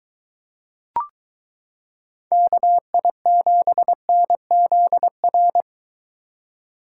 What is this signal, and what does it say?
Morse code, heard as a keyed tone, about 22 wpm, 700 Hz. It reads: KI7NZR